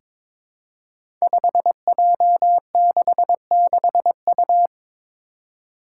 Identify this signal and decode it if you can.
Morse code, heard as a keyed tone, about 23 wpm, 700 Hz. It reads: 5J66U
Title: Five Challenging Letter-Number Combinations 22wpm